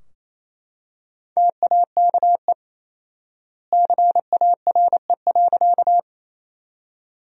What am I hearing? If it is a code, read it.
Morse code, heard as a keyed tone, about 28 wpm, 700 Hz. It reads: TAKE CARE.
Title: 4x Word-Spacing - Sentences from Top 300 Words - 28wpm